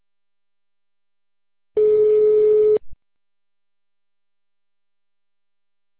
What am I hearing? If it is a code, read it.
Ringback tone (Europe, 425 Hz).